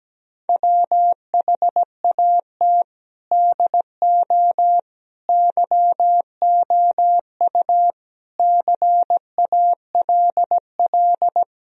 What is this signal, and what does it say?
Morse code, heard as a keyed tone, about 17 wpm, 700 Hz. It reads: WHAT DO YOU CALL
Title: Sentences from Top 1000 Words - Review Part 3 - 17wpm